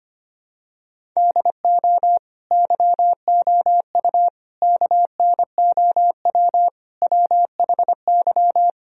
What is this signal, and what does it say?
Morse code, heard as a keyed tone, about 25 wpm, 700 Hz. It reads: DO YOU KNOW WHY
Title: Sentences from Top 1000 Words - Review Part 1 - 25wpm